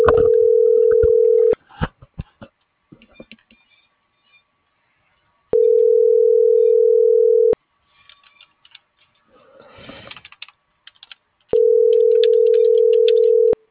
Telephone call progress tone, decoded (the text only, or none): ringback tone